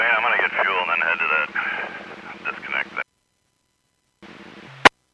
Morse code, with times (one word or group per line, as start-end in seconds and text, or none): none